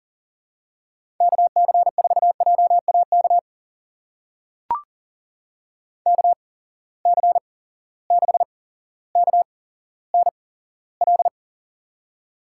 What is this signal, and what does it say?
Morse code, heard as a keyed tone, about 40 wpm, spaced as for 17 wpm, 700 Hz. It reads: KC4JAK KC6KNL
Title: USA Call Signs - 6 Characters - 8x Character Spacing - 40wpm